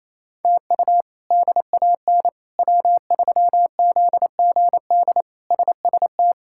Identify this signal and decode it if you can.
Morse code, heard as a keyed tone, about 27 wpm, 700 Hz. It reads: TU DAN W3ZGD SST
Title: K1USN SST QSO Practice - 28wpm